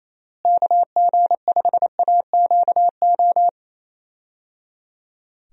Morse code, heard as a keyed tone, about 28 wpm, 700 Hz. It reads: KG5AQO
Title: USA Call Signs - 6 Characters - Repeated 3x - 28wpm